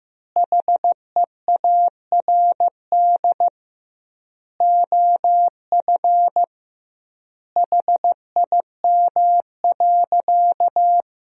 Morse code, heard as a keyed tone, about 16 wpm, 700 Hz. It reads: HEARD OF HIM.
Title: 2x Word-Spacing - Sentences from Top 400 Words - 15wpm